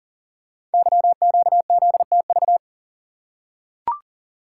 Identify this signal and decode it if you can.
Morse code, heard as a keyed tone, about 41 wpm, 700 Hz. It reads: YQZTV
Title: Code Groups - 5 Letter:Numbers - 40wpm